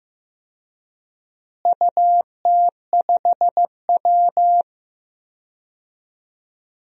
Morse code, heard as a keyed tone, about 15 wpm, 700 Hz. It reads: UT5W
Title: DXCC-100 Call Signs - 4 Characters - Repeated 2x - 15wpm